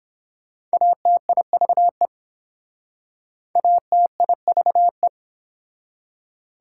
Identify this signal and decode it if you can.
Morse code, heard as a keyed tone, about 27 wpm, 700 Hz. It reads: ATIVE ATIVE
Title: Speed-Racing Prefixes and Suffixes 22wpm